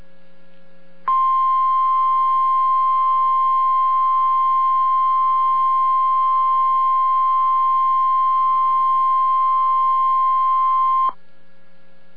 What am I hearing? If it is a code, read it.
Emergency Alert System (SAME) tones (the 1050 Hz attention tone). It is heard from 1.1 to 11.2 s.